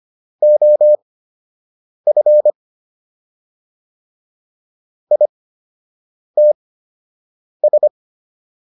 Morse code, heard as a keyed tone, about 25 wpm, spaced as for 7 wpm, 600 Hz. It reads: OF ITS